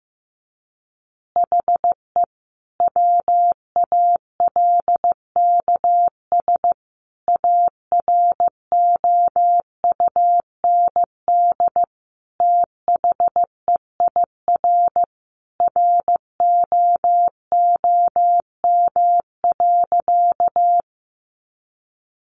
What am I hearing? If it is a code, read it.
Morse code, heard as a keyed tone, about 15 wpm, 700 Hz. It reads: HE WALKS AROUND THEIR ROOM.
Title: Easy Phrases 15wpm